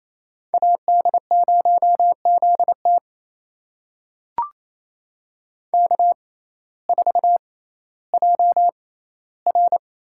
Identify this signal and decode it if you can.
Morse code, heard as a keyed tone, about 28 wpm, spaced as for 15 wpm, 700 Hz. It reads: AD0ZT K4JR
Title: USA Call Signs - 5 Characters - 6x Character Spacing - 28wpm